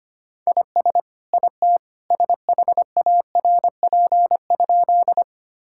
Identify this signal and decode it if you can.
Morse code, heard as a keyed tone, about 25 wpm, 700 Hz. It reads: IS IT SHARP?
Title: Sentences from Top 1600 Words 25wpm